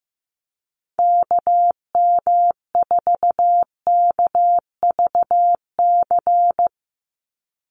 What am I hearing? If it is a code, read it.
Morse code, heard as a keyed tone, about 15 wpm, 700 Hz. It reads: KM4KVC
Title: USA Call Signs 15wpm